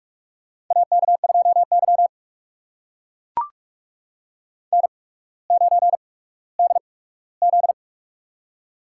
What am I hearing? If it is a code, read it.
Morse code, heard as a keyed tone, about 44 wpm, spaced as for 19 wpm, 700 Hz. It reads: AK2Y N9DZ
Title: USA Call Signs - 4 Characters - 8x Character Spacing - 45wpm